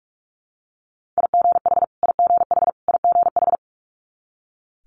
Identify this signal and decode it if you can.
Morse code, heard as a keyed tone, about 46 wpm, 700 Hz. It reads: IGH IGH IGH
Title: Multi-letter Phonemes Repeated 3x 45wpm